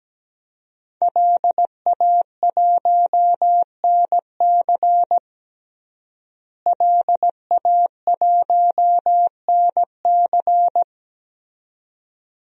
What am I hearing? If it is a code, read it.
Morse code, heard as a keyed tone, about 17 wpm, 700 Hz. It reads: LA1NC LA1NC
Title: DXCC-100 Call Signs - 5 Characters - Repeated 2x - 17wpm